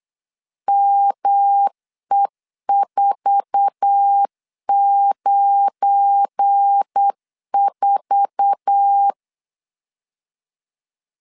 Morse code, heard as a keyed tone, about 8 wpm, 800 Hz. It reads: ME494
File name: LEZIONI / 8 YJ45 / 8 YJ4 8wpm 3246